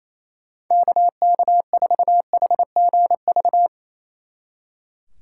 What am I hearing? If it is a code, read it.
Morse code, heard as a keyed tone, about 34 wpm, 700 Hz. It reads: KK4HGV
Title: Speed Racing - USA 2x3 Call Signs 28wpm